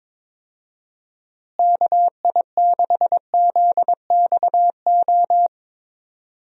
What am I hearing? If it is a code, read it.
Morse code, heard as a keyed tone, about 21 wpm, 700 Hz. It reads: KI6ZXO
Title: USA Call Signs - 6 Characters - Repeated 3x - 22wpm